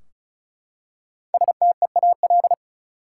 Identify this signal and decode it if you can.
Morse code, heard as a keyed tone, about 36 wpm, 700 Hz. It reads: STEAL
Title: Five Letter Words - 35wpm